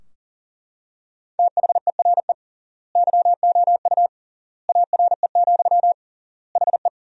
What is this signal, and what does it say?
Morse code, heard as a keyed tone, about 40 wpm, 700 Hz. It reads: THERE YOU ARE, HE
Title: 3x Word-Spacing - Sentences from Top 100 Words - 40wpm